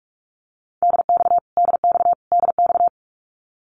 Morse code, heard as a keyed tone, about 44 wpm, 700 Hz. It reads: DX DX DX
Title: Abbreviations Repeated 3x - 45wpm